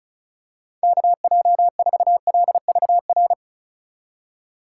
Morse code, heard as a keyed tone, about 34 wpm, 700 Hz. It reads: KJ4LVR
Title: USA Call Signs - 6 Characters - 4x Character Spacing - 35wpm